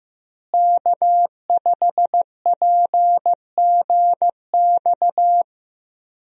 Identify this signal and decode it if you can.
Morse code, heard as a keyed tone, about 15 wpm, 700 Hz. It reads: K5PGX
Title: USA Call Signs - 5 Characters - Repeated 2x - 15wpm